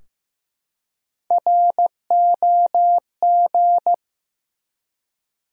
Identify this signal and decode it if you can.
Morse code, heard as a keyed tone, about 15 wpm, 700 Hz. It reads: ROG
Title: USA Common Names - Max 3 Letters - 15wpm